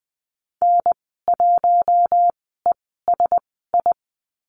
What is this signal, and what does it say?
Morse code, heard as a keyed tone, about 20 wpm, spaced as for 15 wpm, 700 Hz. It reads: N1ESI
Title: 048 Lesson 9g - Number 4 - Call Signs 20wpm